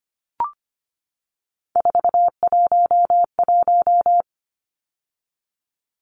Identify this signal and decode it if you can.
Morse code, heard as a keyed tone, about 25 wpm, 700 Hz. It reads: 411
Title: Numbers - 3-Digit 25wpm